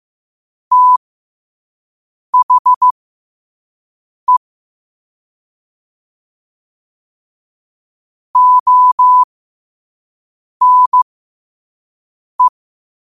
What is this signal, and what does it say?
Morse code, heard as a keyed tone, about 15 wpm, spaced as for 5 wpm, 1000 Hz. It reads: THE ONE
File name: 5wpm